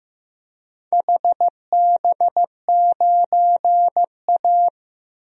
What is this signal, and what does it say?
Morse code, heard as a keyed tone, about 15 wpm, 700 Hz. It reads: HB9A
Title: DXCC-100 Call Signs - 4 Characters - Repeated 2x - 15wpm